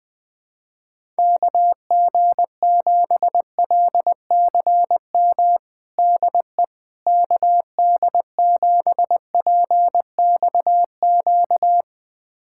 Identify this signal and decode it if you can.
Morse code, heard as a keyed tone, about 20 wpm, 700 Hz. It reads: KG7LCM DE KD7PXQ
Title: USA Callsign DE Callsign - 6 Characters - 20wpm